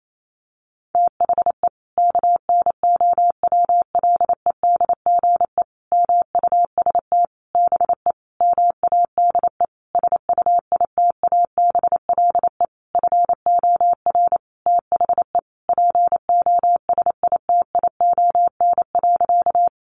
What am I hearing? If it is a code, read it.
Morse code, heard as a keyed tone, about 28 wpm, 700 Hz. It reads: THE KNOWLEDGE MUST BE MADE SUITABLE FOR THE POSITION.
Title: Sentences from Top 1400 Words 28wpm